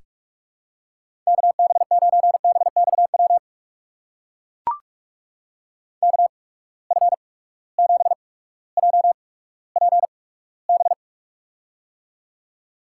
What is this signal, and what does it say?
Morse code, heard as a keyed tone, about 44 wpm, spaced as for 19 wpm, 700 Hz. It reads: KB9BXW KF7JPB
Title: USA Call Signs - 6 Characters - 8x Character Spacing - 45wpm